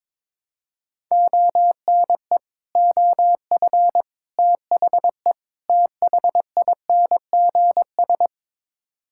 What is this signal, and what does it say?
Morse code, heard as a keyed tone, about 21 wpm, 700 Hz. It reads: ONE OF THE THINGS
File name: practice / 22wpm / Build A Phrase - 5-Word Phrases - 22wpm